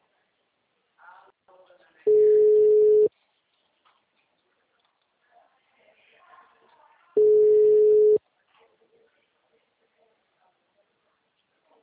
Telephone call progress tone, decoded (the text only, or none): ringback tone